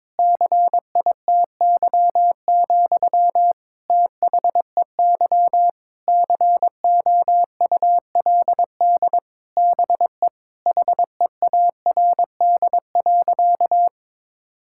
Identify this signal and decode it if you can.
Morse code, heard as a keyed tone, about 22 wpm, 700 Hz. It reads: CITY, THEY COULD BE HEARD.
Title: Sentences from Top 1000 Words - Review Part 5 - 22wpm